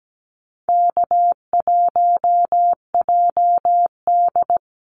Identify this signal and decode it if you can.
Morse code, heard as a keyed tone, about 17 wpm, 700 Hz. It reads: K1JD
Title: NA SOTA Activator Callsigns 17wpm